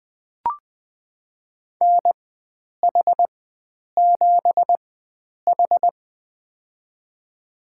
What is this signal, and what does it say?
Morse code, heard as a keyed tone, about 20 wpm, spaced as for 9 wpm, 700 Hz. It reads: NH7H
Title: USA Call Signs - 4 Characters - 4x Character Spacing - 20wpm